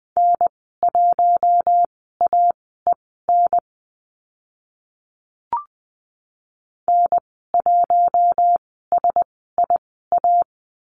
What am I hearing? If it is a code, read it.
Morse code, heard as a keyed tone, about 19 wpm, spaced as for 14 wpm, 700 Hz. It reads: N1AEN N1SIA
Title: 055 Lesson 10g - Letter r - Call Signs 20wpm